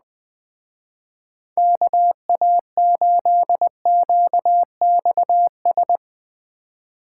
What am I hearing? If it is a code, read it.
Morse code, heard as a keyed tone, about 20 wpm, 700 Hz. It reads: KA8QXS
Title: Speed Racing - USA 2x3 Call Signs 17wpm